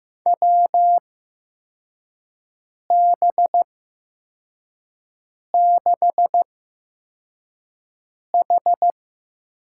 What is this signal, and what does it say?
Morse code, heard as a keyed tone, about 15 wpm, spaced as for 4 wpm, 700 Hz. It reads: WB6H
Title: USA Call Signs - 6 Characters - 8x Character Spacing - 15wpm